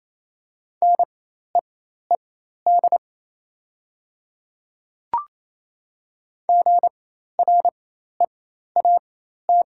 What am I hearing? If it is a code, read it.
Morse code, heard as a keyed tone, about 27 wpm, spaced as for 13 wpm, 700 Hz. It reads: NEED GREAT
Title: 4x Character Spacing - Top 300 Words - 28wpm